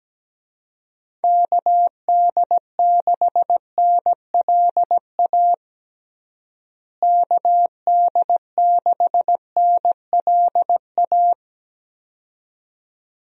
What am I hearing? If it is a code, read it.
Morse code, heard as a keyed tone, about 17 wpm, 700 Hz. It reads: KD6NLA KD6NLA
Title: USA Call Signs - 6 Characters - Repeated 2x - 17wpm